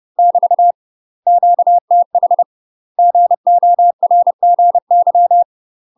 Morse code, heard as a keyed tone, about 30 wpm, 700 Hz. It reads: <BT> QTH GORGY
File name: UK MASTER QSOs TRAINING / 30WPM / UK MASTER QSOs 0011